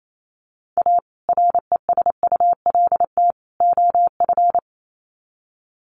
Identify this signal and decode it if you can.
Morse code, heard as a keyed tone, about 28 wpm, 700 Hz. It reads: A RESULT OF
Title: Sets of 3 Words 28wpm